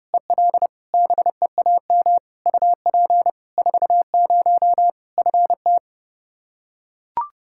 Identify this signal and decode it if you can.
Morse code, heard as a keyed tone, about 32 wpm, 700 Hz. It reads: EL BEAM UP 40 FT
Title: Practice QSO - Exchange 2 - Info - 30wpm